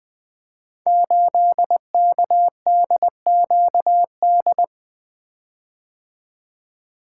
Morse code, heard as a keyed tone, about 20 wpm, 700 Hz. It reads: 8KDQD
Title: Code Groups - 5 Letter:Numbers - 20wpm